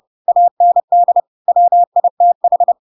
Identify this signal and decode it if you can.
Morse code, heard as a keyed tone, about 30 wpm, 700 Hz. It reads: AND WITH
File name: Pinocchio / Pinocchio ch6 30 30 ik1hge